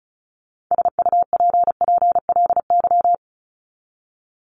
Morse code, heard as a keyed tone, about 34 wpm, 700 Hz. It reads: SUPPLY
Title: Top 501-750 Words 35wpm